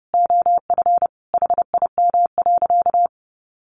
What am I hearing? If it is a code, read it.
Morse code, heard as a keyed tone, about 30 wpm, 700 Hz. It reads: OF HIM.
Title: Sentences from Top 800 Words 30wpm